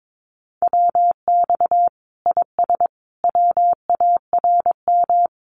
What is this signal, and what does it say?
Morse code, heard as a keyed tone, about 22 wpm, 700 Hz. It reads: WX IS WARM
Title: Common QSO Phrases 22wpm